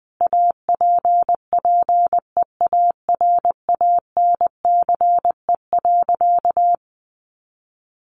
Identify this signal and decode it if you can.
Morse code, heard as a keyed tone, about 20 wpm, 700 Hz. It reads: APPEARANCE.
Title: Sentences from Top 1100 Words 20wpm